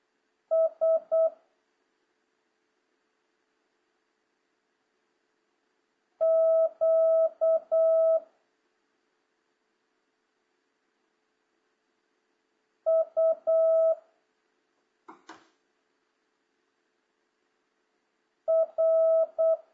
Morse code, heard as a keyed tone, about 8 wpm, 650 Hz. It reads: S Q U R